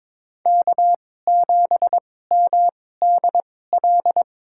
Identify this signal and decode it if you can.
Morse code, heard as a keyed tone, about 22 wpm, spaced as for 16 wpm, 700 Hz. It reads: K7MDL
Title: USA Call Signs - 5 Characters - 2x Character Spacing - 22wpm